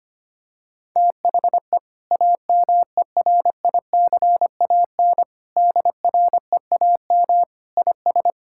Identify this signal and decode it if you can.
Morse code, heard as a keyed tone, about 25 wpm, 690 Hz. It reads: THE AMERICAN DREAM IS
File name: practice / 25wpm / Sentences with 2-Word Phrases - 25wpm